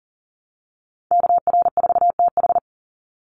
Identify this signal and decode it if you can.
Morse code, heard as a keyed tone, about 40 wpm, 700 Hz. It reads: KR4TH
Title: Call Signs - 40wpm